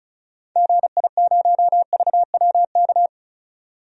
Morse code, heard as a keyed tone, about 34 wpm, 700 Hz. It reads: GI0VWK
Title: DXCC-100 Call Signs - 6 Characters - 35wpm